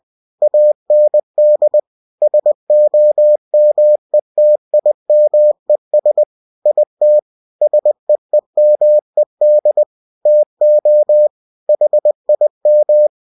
Morse code, heard as a keyed tone, about 20 wpm, 600 Hz. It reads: AND SOMETIMES IT SEEMED TO HIM